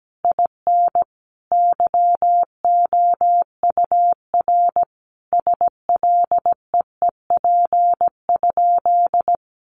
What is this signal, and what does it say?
Morse code, heard as a keyed tone, about 17 wpm, 700 Hz. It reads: IN YOUR SLEEP?
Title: Sentences from Top 500 Words 17wpm